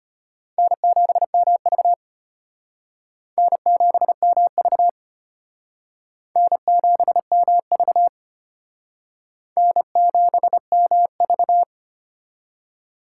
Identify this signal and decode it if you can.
Morse code, heard as a keyed tone, about 33 wpm, 700 Hz. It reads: N7MV N7MV N7MV N7MV